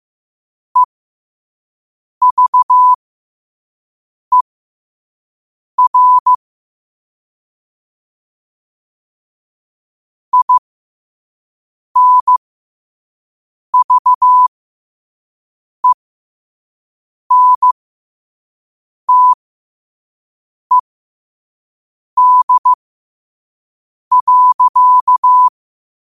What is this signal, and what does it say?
Morse code, heard as a keyed tone, about 15 wpm, spaced as for 5 wpm, 1000 Hz. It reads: EVER INVENTED.
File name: qotdfiles / 5wpm